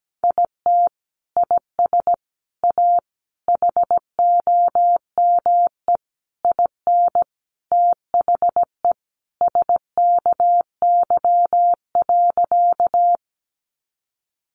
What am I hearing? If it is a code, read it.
Morse code, heard as a keyed tone, about 17 wpm, 700 Hz. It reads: IT IS A HOME IN THE SKY.
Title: Sentences from Top 1200 Words 17wpm